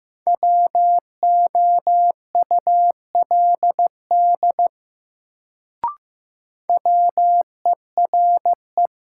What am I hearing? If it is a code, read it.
Morse code, heard as a keyed tone, about 15 wpm, 700 Hz. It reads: WOULD WERE
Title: Sets of 2 Words - New Word Drill - 15wpm